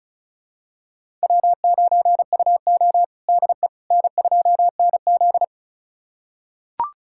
Morse code, heard as a keyed tone, about 36 wpm, 700 Hz. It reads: W9UO DE N2NZ
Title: USA Callsign DE Callsign - 4 Characters - 35wpm